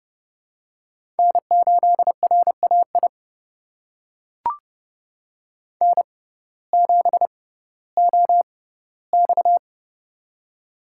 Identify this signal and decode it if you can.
Morse code, heard as a keyed tone, about 30 wpm, spaced as for 17 wpm, 700 Hz. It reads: N8RAI N7OX
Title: USA Call Signs - 6x Character Spacing - 30wpm